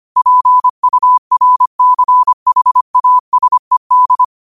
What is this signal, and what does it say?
Morse code, heard as a keyed tone, about 25 wpm, 1000 Hz. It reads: PURCHASED
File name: qotdfiles / 25wpm